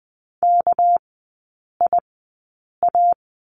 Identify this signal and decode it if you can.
Morse code, heard as a keyed tone, about 20 wpm, spaced as for 8 wpm, 700 Hz. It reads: KIA
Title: 198 Lesson 32e - Letter k - Three Characters 20wpm